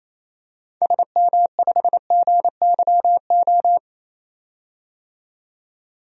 Morse code, heard as a keyed tone, about 28 wpm, 700 Hz. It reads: SM5GYO
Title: DXCC-100 Call Signs - 6 Characters - 28wpm